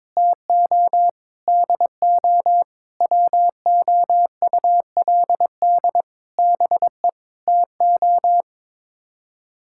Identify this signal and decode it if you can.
Morse code, heard as a keyed tone, about 22 wpm, 700 Hz. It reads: TO DO WOULD BE TO
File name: practice / 22wpm / Sets of 8 Words - 22wpm